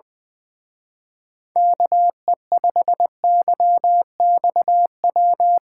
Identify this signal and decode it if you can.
Morse code, heard as a keyed tone, about 21 wpm, 700 Hz. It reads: KE5YXW
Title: Speed Racing - USA 2x3 Call Signs 20wpm